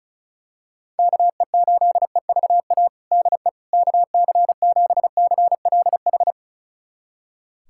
Morse code, heard as a keyed tone, about 36 wpm, 700 Hz. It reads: KE8EVA DE KC7CLH
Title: USA Callsign DE Callsign 35wpm